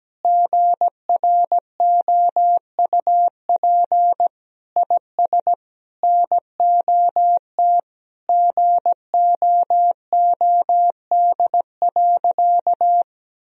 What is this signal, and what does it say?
Morse code, heard as a keyed tone, about 17 wpm, 700 Hz. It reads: GROUP IS NOT GOOD.
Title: Sentences from Top 1000 Words - Review Part 1 - 17wpm